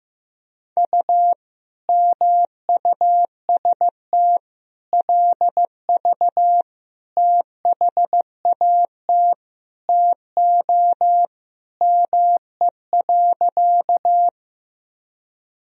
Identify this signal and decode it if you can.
Morse code, heard as a keyed tone, about 16 wpm, 700 Hz. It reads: U MUST LV THAT TO ME.
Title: Sentences from Top 400 Words and Abbreviations 15wpm